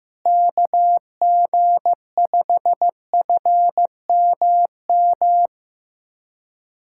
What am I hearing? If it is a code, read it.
Morse code, heard as a keyed tone, about 15 wpm, 700 Hz. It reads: KG5FMM
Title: USA Callsign DE Callsign - 6 Characters - 15wpm